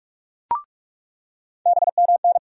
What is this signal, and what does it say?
Morse code, heard as a keyed tone, about 46 wpm, 700 Hz. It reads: BMN
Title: USA Call Sign Postfixes - 3 Letters 45wpm